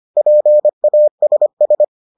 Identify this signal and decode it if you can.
Morse code, heard as a keyed tone, about 25 wpm, 600 Hz. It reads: PASS